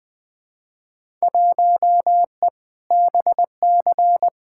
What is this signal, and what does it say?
Morse code, heard as a keyed tone, about 20 wpm, 700 Hz. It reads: 1E BC
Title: Field Day Exchanges - 20wpm